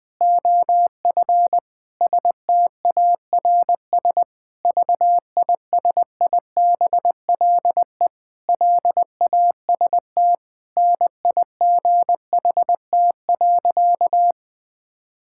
Morse code, heard as a keyed tone, about 20 wpm, 700 Hz. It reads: OF STARS VISIBLE LAST NIGHT.
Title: Sentences with 5-Word Phrases - Spoken Beginning - 20wpm